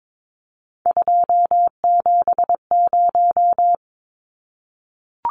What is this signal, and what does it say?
Morse code, heard as a keyed tone, about 22 wpm, 700 Hz. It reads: 270
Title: Numbers - 3-Digit 22wpm